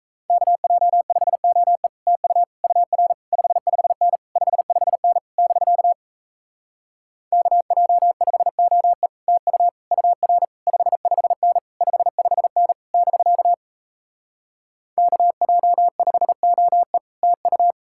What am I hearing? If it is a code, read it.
Morse code, heard as a keyed tone, about 38 wpm, 700 Hz. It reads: KJ5OE TU UR 55N 55N <BK> KJ5OE TU UR 55N 55N <BK> KJ5OE TU